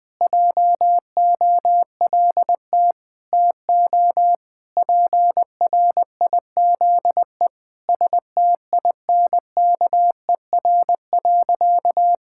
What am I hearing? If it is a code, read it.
Morse code, heard as a keyed tone, about 20 wpm, 700 Hz. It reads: JOLT TO PRIZE STINKER.
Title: Pangram Challenge v2 - 20wpm